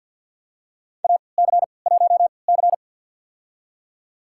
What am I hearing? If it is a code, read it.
Morse code, heard as a keyed tone, about 51 wpm, spaced as for 28 wpm, 700 Hz. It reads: AC1C
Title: USA Call Signs - 4 Characters - 3x Character Spacing - 50wpm